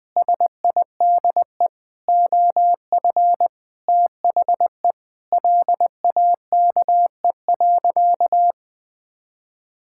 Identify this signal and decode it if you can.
Morse code, heard as a keyed tone, about 20 wpm, 700 Hz. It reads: SIDE OF THE LAKE.
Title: Sentences with 5-Word Phrases - 20wpm